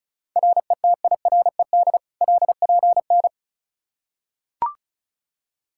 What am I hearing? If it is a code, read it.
Morse code, heard as a keyed tone, about 34 wpm, 700 Hz. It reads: RETIRED LPN
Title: Practice QSO - Exchange 3 - More Info - 35wpm